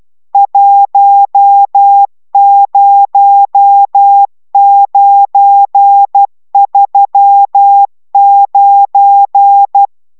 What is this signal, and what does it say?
Morse code, heard as a keyed tone, about 12 wpm, 800 Hz. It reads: 10939